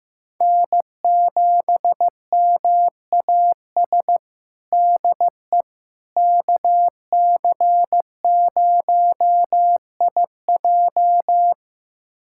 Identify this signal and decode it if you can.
Morse code, heard as a keyed tone, about 15 wpm, 700 Hz. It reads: N7MAS DE KC0IJ
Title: USA Callsign DE Callsign - 5 Characters - 15wpm